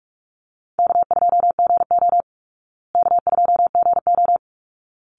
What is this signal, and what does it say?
Morse code, heard as a keyed tone, about 45 wpm, 700 Hz. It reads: K2GO K2GO
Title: Easy Call Signs Repeated 2x - 45wpm